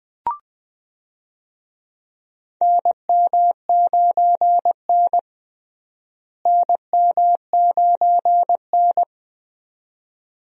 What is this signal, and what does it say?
Morse code, heard as a keyed tone, about 21 wpm, 700 Hz. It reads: NM9N NM9N
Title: USA Call Signs - 4 Characters - Repeated 2x - 20wpm